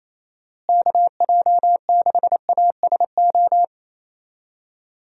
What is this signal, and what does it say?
Morse code, heard as a keyed tone, about 27 wpm, 700 Hz. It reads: KJ6ASO
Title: USA Call Signs - 6 Characters - Repeated 3x - 28wpm